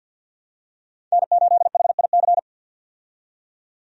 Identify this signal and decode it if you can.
Morse code, heard as a keyed tone, about 49 wpm, 700 Hz. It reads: N8HIC
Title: USA Call Signs - 5 Characters - 50wpm